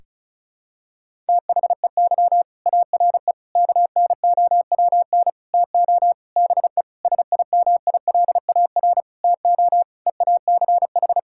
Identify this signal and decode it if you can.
Morse code, heard as a keyed tone, about 34 wpm, 700 Hz. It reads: THEY ARE KNOWN TO BE SIMILAR TO EACH
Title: Sentences from Top 1000 Words - Review Part 5 - 35wpm